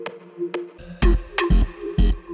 There is no code.